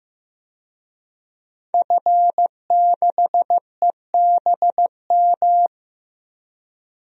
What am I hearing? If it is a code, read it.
Morse code, heard as a keyed tone, about 16 wpm, 700 Hz. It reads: F6EBM
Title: DXCC-100 Call Signs - 5 Characters - 15wpm